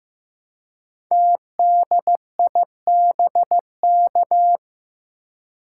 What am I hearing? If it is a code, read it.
Morse code, heard as a keyed tone, about 15 wpm, 700 Hz. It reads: TDIBK
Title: Code Groups - 5 Letters - 15wpm